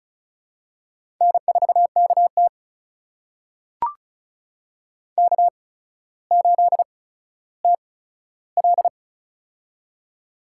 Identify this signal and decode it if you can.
Morse code, heard as a keyed tone, about 36 wpm, spaced as for 15 wpm, 700 Hz. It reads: N4KT K8TL
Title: USA Call Signs - 4 Characters - 8x Character Spacing - 35wpm